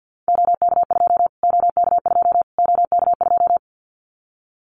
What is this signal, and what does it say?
Morse code, heard as a keyed tone, about 50 wpm, 700 Hz. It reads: OK2 OK2 OK2
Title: Callsign Prefixes for Top 25 DXCC - Repeated 3x - 50wpm